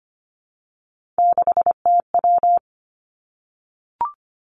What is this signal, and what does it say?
Morse code, heard as a keyed tone, about 25 wpm, 700 Hz. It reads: 6TW
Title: Three Challenging Letter-Number Combinations 25wpm